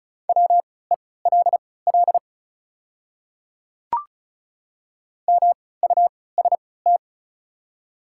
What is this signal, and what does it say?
Morse code, heard as a keyed tone, about 36 wpm, spaced as for 20 wpm, 700 Hz. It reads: WELL MUST
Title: 3x Character Spacing - Top 300 Words - 35wpm